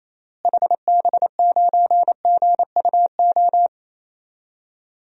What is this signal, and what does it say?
Morse code, heard as a keyed tone, about 28 wpm, 700 Hz. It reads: HB9GUO
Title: DXCC-100 Call Signs - 6 Characters - Repeated 3x - 28wpm